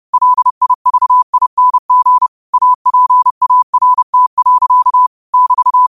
Morse code, heard as a keyed tone, about 30 wpm, 1000 Hz. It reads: LIVING APART. <BT>
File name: qotdfiles / 30wpm